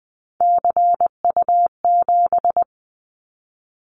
Morse code, heard as a keyed tone, about 20 wpm, 700 Hz. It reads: CU7
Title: Callsign Prefixes for Top 100 DXCC - 20wpm